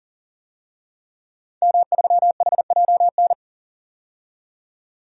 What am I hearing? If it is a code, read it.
Morse code, heard as a keyed tone, about 38 wpm, 700 Hz. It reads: M3HJN
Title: DXCC-100 Call Signs - 5 Characters - 40wpm